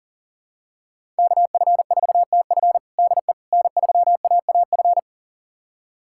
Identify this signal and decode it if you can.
Morse code, heard as a keyed tone, about 40 wpm, 700 Hz. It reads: KF4TF DE N3AAF
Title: USA Callsign DE Callsign - 5 Characters - 40wpm